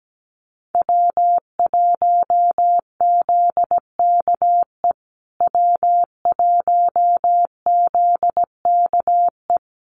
Morse code, heard as a keyed tone, about 17 wpm, 700 Hz. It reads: W1ZKE W1ZKE
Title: USA Call Signs Repeated 3x 17wpm